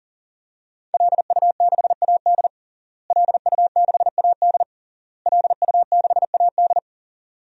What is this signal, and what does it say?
Morse code, heard as a keyed tone, about 40 wpm, 700 Hz. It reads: LU6AD LU6AD LU6AD
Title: DXCC-100 Call Signs - 5 Characters - Repeated 3x - 40wpm